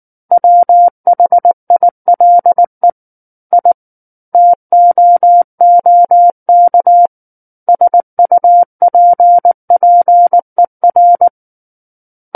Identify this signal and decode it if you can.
Morse code, heard as a keyed tone, about 19 wpm, 700 Hz. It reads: WHILE I TOOK SUPPER
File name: Book One / 10-Chapter 10 - 19 WPM